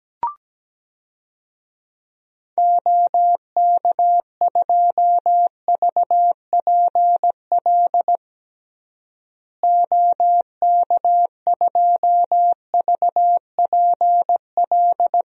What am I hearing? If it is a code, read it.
Morse code, heard as a keyed tone, about 17 wpm, 700 Hz. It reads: OK2VPL OK2VPL
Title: DXCC-100 Call Signs - 6 Characters - Repeated 2x - 17wpm